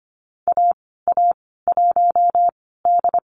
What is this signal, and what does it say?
Morse code, heard as a keyed tone, about 25 wpm, 700 Hz. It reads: A A 1 D
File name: Lesson 15g - Number 5 - Call Signs 25wpm